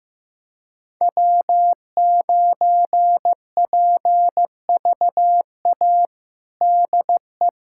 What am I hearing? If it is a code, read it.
Morse code, heard as a keyed tone, about 15 wpm, 700 Hz. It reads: W9PVA DE
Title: USA Callsign DE Callsign - 5 Characters - 15wpm